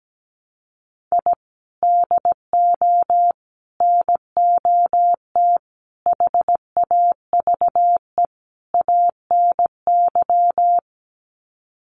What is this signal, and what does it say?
Morse code, heard as a keyed tone, about 17 wpm, 700 Hz. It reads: I DO NOT HAVE ANY
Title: Sets of 5 Words 17wpm